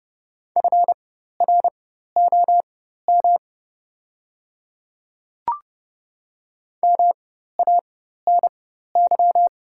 Morse code, heard as a keyed tone, about 30 wpm, spaced as for 14 wpm, 690 Hz. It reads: FROM MANY
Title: 4x Character Spacing - Top 100 Words - 30wpm